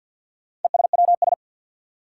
Morse code, heard as a keyed tone, about 49 wpm, 700 Hz. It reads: ESWS
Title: Four Challenging Letter-Number Combinations 50wpm